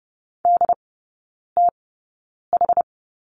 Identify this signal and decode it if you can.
Morse code, heard as a keyed tone, about 30 wpm, spaced as for 9 wpm, 700 Hz. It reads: DTH
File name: 081 Lesson 14e - Number 2 - Three Characters 30wpm